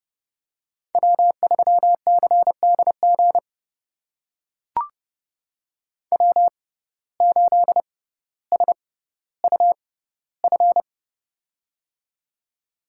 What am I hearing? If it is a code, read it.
Morse code, heard as a keyed tone, about 32 wpm, spaced as for 16 wpm, 700 Hz. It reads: W3CDG W8SUF
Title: USA Call Signs - 5 Characters - 6x Character Spacing - 30wpm